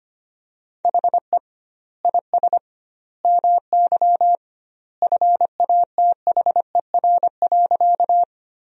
Morse code, heard as a keyed tone, about 25 wpm, 700 Hz. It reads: HE IS MY FATHER.
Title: 2x Word-Spacing - Sentences from Top 300 Words - 25wpm